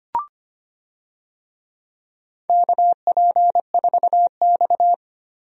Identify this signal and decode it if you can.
Morse code, heard as a keyed tone, about 25 wpm, 700 Hz. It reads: KP4X
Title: DXCC-100 Call Signs - 4 Characters - Repeated 3x - 25wpm